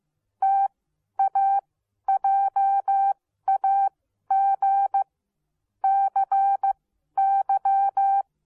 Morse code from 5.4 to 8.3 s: CY